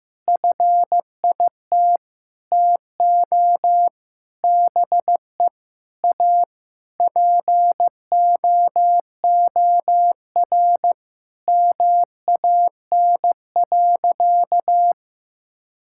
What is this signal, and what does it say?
Morse code, heard as a keyed tone, about 16 wpm, 700 Hz. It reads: FIT TO BE A POOR MAN.
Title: Sentences from Top 1000 Words - Review Part 5 - 15wpm